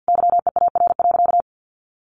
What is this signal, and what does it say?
Morse code, heard as a keyed tone, about 50 wpm, 700 Hz. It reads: YEAR.
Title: Sentences from Top 300 Words 50wpm